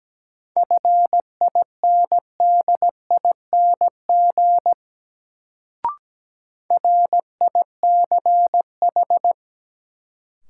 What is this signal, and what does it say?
Morse code, heard as a keyed tone, about 17 wpm, 700 Hz. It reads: FINDING RICH
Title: Sentences from Top 1100 Words - New Word Drill - 17wpm